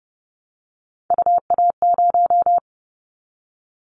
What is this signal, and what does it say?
Morse code, heard as a keyed tone, about 30 wpm, 700 Hz. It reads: UA0
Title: Callsign Prefixes for Top 25 DXCC - Repeated 3x - 30wpm